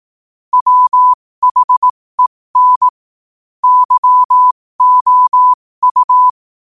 Morse code, heard as a keyed tone, about 18 wpm, spaced as for 15 wpm, 1000 Hz. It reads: WHEN YOU
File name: qotdfiles / 15wpm